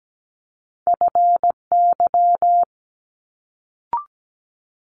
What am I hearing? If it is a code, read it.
Morse code, heard as a keyed tone, about 17 wpm, 700 Hz. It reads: FY
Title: Multi-letter Phonemes Quick Fire 17wpm